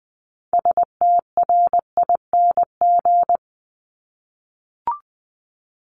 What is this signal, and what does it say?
Morse code, heard as a keyed tone, about 20 wpm, 700 Hz. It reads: STRING
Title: Multi-letter Phonemes 20wpm